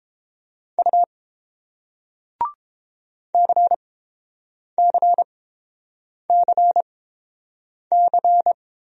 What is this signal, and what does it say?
Morse code, heard as a keyed tone, about 30 wpm, spaced as for 7 wpm, 690 Hz. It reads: U CCCC